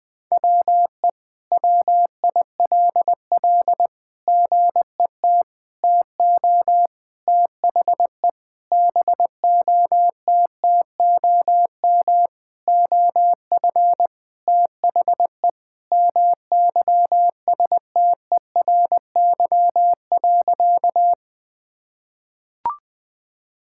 Morse code, heard as a keyed tone, about 21 wpm, 700 Hz. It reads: WE WILL GET TO THE BOTTOM OF THE MYSTERY.
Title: Sentences with 6-Word Phrases - 20wpm